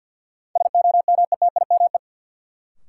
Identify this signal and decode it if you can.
Morse code, heard as a keyed tone, about 53 wpm, 700 Hz. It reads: SOMETIME
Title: Sentences from Top 1500 Words - New Word Drill Alt - 50wpm